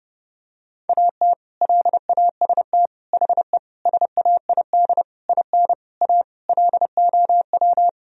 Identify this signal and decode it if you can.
Morse code, heard as a keyed tone, about 30 wpm, 700 Hz. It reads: AT LAST HE SAID IN A LOW
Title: Sentences from Top 1000 Words - Review Part 1 - 30wpm